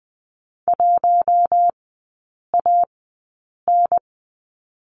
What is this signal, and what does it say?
Morse code, heard as a keyed tone, about 20 wpm, spaced as for 8 wpm, 700 Hz. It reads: 1AN
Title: 041 Lesson 8e - Number 1 - Three Characters 20wpm